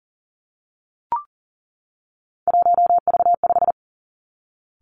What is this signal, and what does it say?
Morse code, heard as a keyed tone, about 41 wpm, 700 Hz. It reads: 1V5
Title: Three Challenging Letter-Number Combinations 40wpm